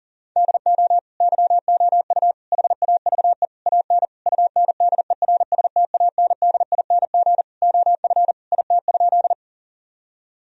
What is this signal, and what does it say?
Morse code, heard as a keyed tone, about 40 wpm, 700 Hz. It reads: DO YOU HAVE AN UNDERSTANDING OF IT?
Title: Sentences from Top 1000 Words - Review Part 3 - 40wpm